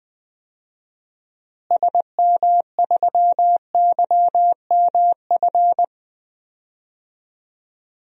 Morse code, heard as a keyed tone, about 20 wpm, 700 Hz. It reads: SM3YMF
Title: DXCC-100 Call Signs - 6 Characters - Repeated 2x - 20wpm